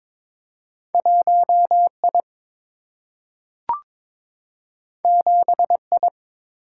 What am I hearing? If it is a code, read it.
Morse code, heard as a keyed tone, about 22 wpm, 700 Hz. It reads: 1 I 7 I
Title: Single-Letter Single-Number 22wpm